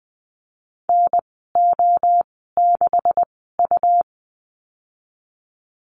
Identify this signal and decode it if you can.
Morse code, heard as a keyed tone, about 20 wpm, spaced as for 15 wpm, 700 Hz. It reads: NO6U
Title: 140 Lesson 23g - Letter f - Call Signs 20wpm